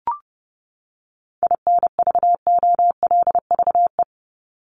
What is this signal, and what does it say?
Morse code, heard as a keyed tone, about 30 wpm, 700 Hz. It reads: INVOLVE
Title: ING Suffix - Top 100 Words with Base Words 30wpm